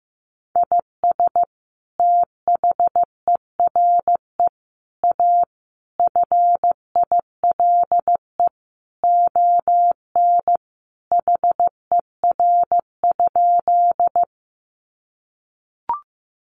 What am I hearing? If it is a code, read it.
Morse code, heard as a keyed tone, about 15 wpm, 700 Hz. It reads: IS THERE A FILE ON HER?
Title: Sentences from Top 1200 Words 15wpm